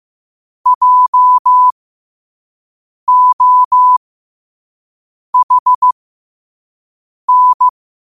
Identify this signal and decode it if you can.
Morse code, heard as a keyed tone, about 15 wpm, spaced as for 5 wpm, 1000 Hz. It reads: JOHN
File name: qotdfiles / 5wpm